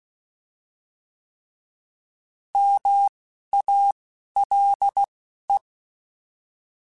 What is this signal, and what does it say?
Morse code, heard as a keyed tone, about 16 wpm, spaced as for 11 wpm, 800 Hz. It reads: MALE
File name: Alphabetical / Alphabetical mp3s / Ms @10